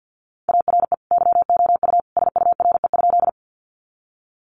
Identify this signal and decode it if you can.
Morse code, heard as a keyed tone, about 50 wpm, 700 Hz. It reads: ARE YOU SURE?
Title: Sentences from Top 300 Words 50wpm